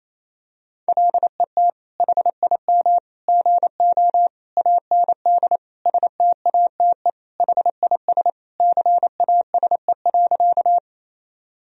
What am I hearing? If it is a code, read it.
Morse code, heard as a keyed tone, about 28 wpm, 700 Hz. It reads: LET HIM GO AND STATE HIS CASE.
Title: Sentences from Top 1000 Words - Review Part 2 - 28wpm